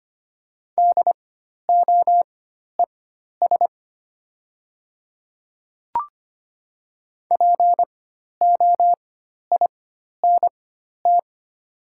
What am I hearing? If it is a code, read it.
Morse code, heard as a keyed tone, about 25 wpm, spaced as for 12 wpm, 700 Hz. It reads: DOES POINT
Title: 4x Character Spacing - Top 200 Words - 25wpm